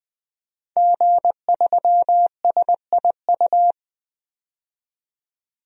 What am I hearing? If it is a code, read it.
Morse code, heard as a keyed tone, about 20 wpm, 700 Hz. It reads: G3SIU
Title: DXCC-100 Call Signs - 5 Characters - 20wpm